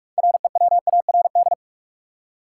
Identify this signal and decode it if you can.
Morse code, heard as a keyed tone, about 46 wpm, 700 Hz. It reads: REWARD
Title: Six Letter Words - 45wpm